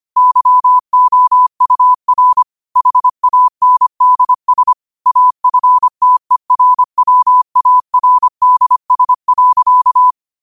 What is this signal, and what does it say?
Morse code, heard as a keyed tone, about 25 wpm, 1000 Hz. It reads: YOUR HANDS AFTERWARDS.
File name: qotdfiles / 25wpm